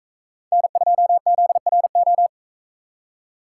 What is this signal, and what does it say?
Morse code, heard as a keyed tone, about 41 wpm, 700 Hz. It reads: N2ZRO